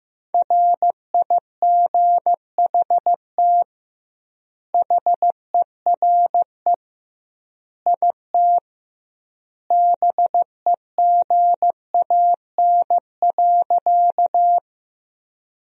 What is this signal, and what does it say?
Morse code, heard as a keyed tone, about 15 wpm, 700 Hz. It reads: RIGHT HERE IT BEGAN.
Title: 2x Word-Spacing - Sentences from Top 300 Words - 15wpm